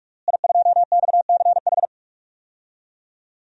Morse code, heard as a keyed tone, about 41 wpm, 700 Hz. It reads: I2XCH
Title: DXCC-100 Call Signs - 5 Characters - Repeated 3x - 45wpm